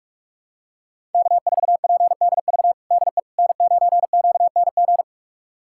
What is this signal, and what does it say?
Morse code, heard as a keyed tone, about 46 wpm, 700 Hz. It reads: K4PDV DE N9QNG
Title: USA Callsign DE Callsign 45wpm